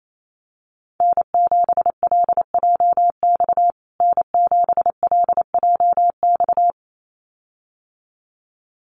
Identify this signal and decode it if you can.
Morse code, heard as a keyed tone, about 28 wpm, 700 Hz. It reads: N7LJX N7LJX
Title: USA Call Signs Repeated 2x 28wpm